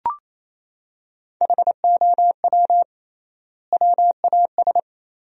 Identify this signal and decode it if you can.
Morse code, heard as a keyed tone, about 28 wpm, 700 Hz. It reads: HOW WAS
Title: 3x Word-Spacing - Sentences from Top 100 Words - 28wpm